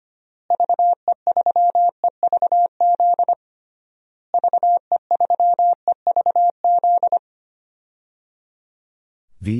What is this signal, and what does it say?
Morse code, heard as a keyed tone, about 25 wpm, 700 Hz. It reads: VE3EVZ VE3EVZ
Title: DXCC-100 Call Signs - 6 Characters - Repeated 3x - 25wpm